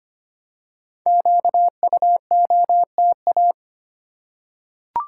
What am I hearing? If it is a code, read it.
Morse code, heard as a keyed tone, about 25 wpm, 700 Hz. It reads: QUOTA
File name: 5 Letter Words - First 3 Letters Given - 25wpm